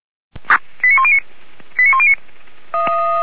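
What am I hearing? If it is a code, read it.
Five-tone selective call (ZVEI), heard as quick runs of five tones, tones of about 70 ms. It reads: 89109 89109
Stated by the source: ZVEI five-tone selective call received with an SDR.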